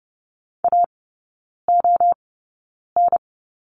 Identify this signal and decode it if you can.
Morse code, heard as a keyed tone, about 30 wpm, spaced as for 9 wpm, 690 Hz. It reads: AON
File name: Lesson 16e - Letter c - Three Characters 30wpm